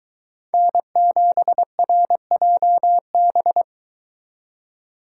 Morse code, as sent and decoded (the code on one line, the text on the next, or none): -. --... .-. .--- -...
N7RJB